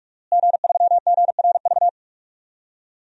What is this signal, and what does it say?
Morse code, heard as a keyed tone, about 43 wpm, 700 Hz. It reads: G3GRV
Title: DXCC-100 Call Signs - 5 Characters - Repeated 3x - 45wpm